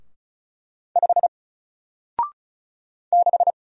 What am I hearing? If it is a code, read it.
Morse code, heard as a keyed tone, about 36 wpm, 700 Hz. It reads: 5 6
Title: Single Letter-Number Alt 35wpm